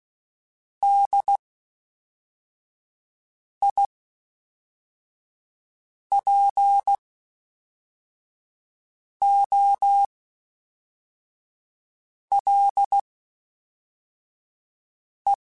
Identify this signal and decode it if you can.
Morse code, heard as a keyed tone, about 16 wpm, spaced as for 4 wpm, 800 Hz. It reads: DIPOLE